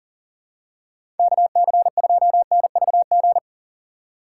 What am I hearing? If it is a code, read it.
Morse code, heard as a keyed tone, about 40 wpm, 700 Hz. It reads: KC2NVG
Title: USA Call Signs - 6 Characters - 6x Character Spacing - 40wpm